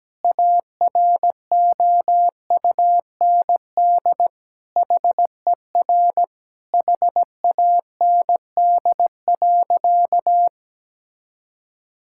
Morse code, heard as a keyed tone, about 17 wpm, 700 Hz. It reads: AROUND HER HAND.
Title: Sentences from Top 1600 Words 17wpm